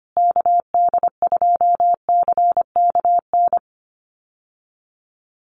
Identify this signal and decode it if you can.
Morse code, heard as a keyed tone, about 25 wpm, 700 Hz. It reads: KD2CKN
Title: USA Call Signs Repeated 2x 25wpm